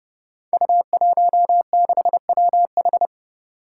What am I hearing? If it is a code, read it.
Morse code, heard as a keyed tone, about 30 wpm, 700 Hz. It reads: U16WH
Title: Five Challenging Letter-Number Combinations 30wpm